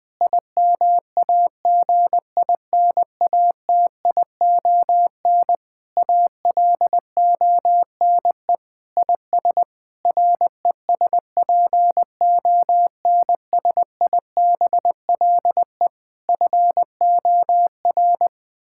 Morse code, heard as a keyed tone, about 20 wpm, 700 Hz. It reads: IMAGINATION ALONE IS RESPONSIBLE FOR
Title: Sentences from Top 1600 Words 20wpm